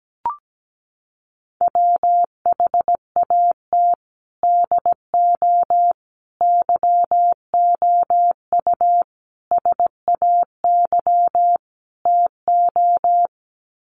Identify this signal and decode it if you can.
Morse code, heard as a keyed tone, about 17 wpm, 700 Hz. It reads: WHAT DO YOU SAY TO
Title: Sets of 5 Words 17wpm